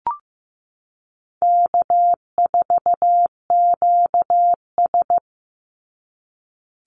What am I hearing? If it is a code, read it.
Morse code, heard as a keyed tone, about 15 wpm, 700 Hz. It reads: K4QS
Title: NA SOTA Chaser Callsigns 15wpm